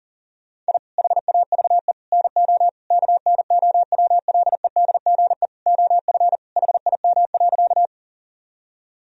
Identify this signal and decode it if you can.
Morse code, heard as a keyed tone, about 40 wpm, 700 Hz. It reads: I HAVE NO KNOWLEDGE OF HIM.
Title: Sentences from Top 1000 Words - Review Part 3 - 40wpm